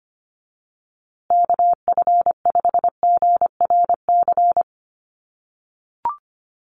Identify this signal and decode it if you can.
Morse code, heard as a keyed tone, about 25 wpm, 700 Hz. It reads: KF5GRC
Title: USA Call Signs Repeated 2x 25wpm